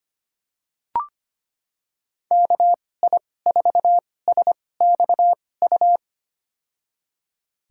Morse code, heard as a keyed tone, about 25 wpm, spaced as for 18 wpm, 700 Hz. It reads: KI4SXU
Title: USA Call Signs - 6 Characters - 2x Character Spacing - 25wpm